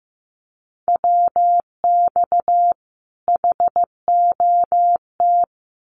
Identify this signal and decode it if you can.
Morse code, heard as a keyed tone, about 15 wpm, 700 Hz. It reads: WX HOT
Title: Common QSO Phrases 15wpm